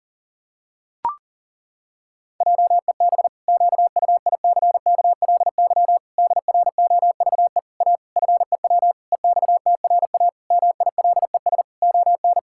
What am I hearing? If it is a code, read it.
Morse code, heard as a keyed tone, about 40 wpm, 700 Hz. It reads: JEB QUICKLY DROVE A FEW EXTRA MILES ON
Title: Pangram Challenge v2 - 40wpm